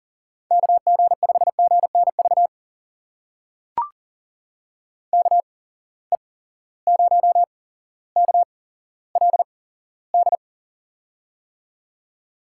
Morse code, heard as a keyed tone, about 40 wpm, spaced as for 17 wpm, 700 Hz. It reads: KG5GNV KE0KLD
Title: USA Call Signs - 6 Characters - 8x Character Spacing - 40wpm